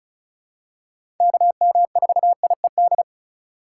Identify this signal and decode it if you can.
Morse code, heard as a keyed tone, about 36 wpm, 700 Hz. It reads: KM4IED
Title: USA Call Signs - 6 Characters - 3x Character Spacing - 35wpm